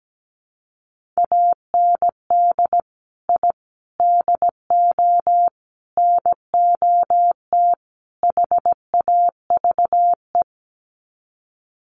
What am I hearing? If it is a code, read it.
Morse code, heard as a keyed tone, about 17 wpm, 700 Hz. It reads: AND I DO NOT HAVE
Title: Sets of 5 Words Encore 17wpm